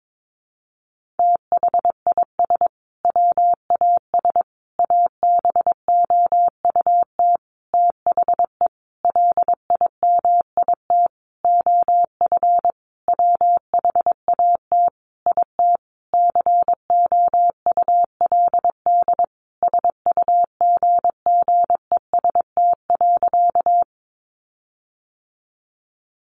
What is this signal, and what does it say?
Morse code, heard as a keyed tone, about 22 wpm, 700 Hz. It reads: THIS WAS ABOUT THE LIMIT OF WHAT IT COULD SUGGEST.
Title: Sentences from Top 1500 Words 22wpm